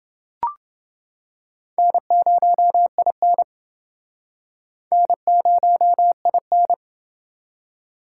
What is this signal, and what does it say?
Morse code, heard as a keyed tone, about 28 wpm, 700 Hz. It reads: N0IN N0IN